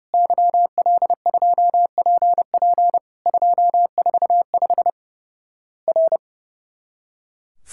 Morse code, heard as a keyed tone, about 32 wpm, 700 Hz. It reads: YL2PP 245 R
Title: Worked All Europe DX Contest - QTC Exchange - 30wpm